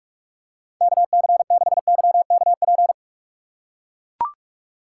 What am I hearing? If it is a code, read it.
Morse code, heard as a keyed tone, about 42 wpm, 700 Hz. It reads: KC6YKP
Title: USA Call Signs - 6 Characters - 6x Character Spacing - 45wpm